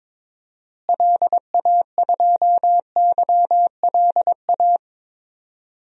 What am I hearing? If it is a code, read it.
Morse code, heard as a keyed tone, about 22 wpm, 700 Hz. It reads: LA2YLA
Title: DXCC-100 Call Signs - 6 Characters - Repeated 3x - 22wpm